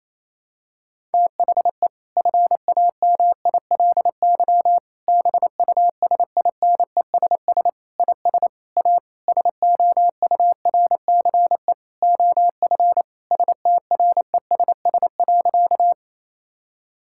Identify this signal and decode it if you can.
Morse code, heard as a keyed tone, about 28 wpm, 700 Hz. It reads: THE FAMILY BUSINESS IS A SOURCE OF STRESS.
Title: Sentences from Top 1000 Words - Review Part 6 - 28wpm